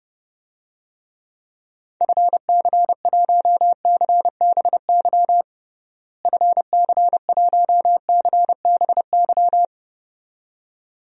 Morse code, heard as a keyed tone, about 30 wpm, 700 Hz. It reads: FC1CBY FC1CBY
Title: DXCC-100 Call Signs - 6 Characters - Repeated 2x - 30wpm